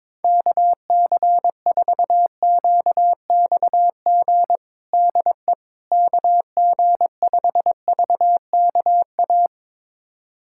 Morse code, heard as a keyed tone, about 22 wpm, 700 Hz. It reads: KC4QXG DE KG5VKA
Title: USA Callsign DE Callsign - 6 Characters - 22wpm